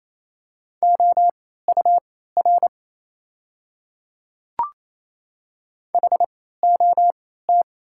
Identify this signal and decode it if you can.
Morse code, heard as a keyed tone, about 28 wpm, spaced as for 16 wpm, 700 Hz. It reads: OUR HOT
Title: 3x Character Spacing - Top 300 Words - 28wpm